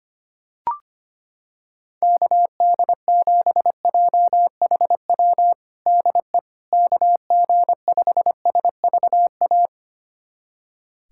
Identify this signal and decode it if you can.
Morse code, heard as a keyed tone, about 25 wpm, 700 Hz. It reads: KD7JHW DE KG5SVA
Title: USA Callsign DE Callsign - 6 Characters - 25wpm